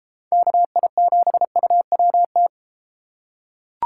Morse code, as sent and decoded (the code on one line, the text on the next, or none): -.- .. --... ..- .-- -
KI7UWT